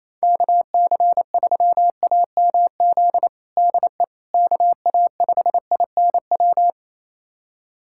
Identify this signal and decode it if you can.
Morse code, heard as a keyed tone, about 28 wpm, 700 Hz. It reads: KC3AMZ DE KA5INW
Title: USA Callsign DE Callsign - 6 Characters - 28wpm